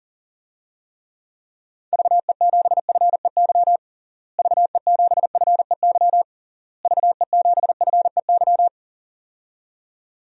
Morse code, heard as a keyed tone, about 40 wpm, 700 Hz. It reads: VE7FEY VE7FEY VE7FEY
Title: DXCC-100 Call Signs - 6 Characters - Repeated 3x - 40wpm